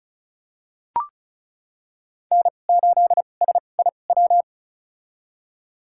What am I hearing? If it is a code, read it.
Morse code, heard as a keyed tone, about 36 wpm, spaced as for 25 wpm, 700 Hz. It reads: N8SIW
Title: USA Call Signs - 5 Characters - 2x Character Spacing - 35wpm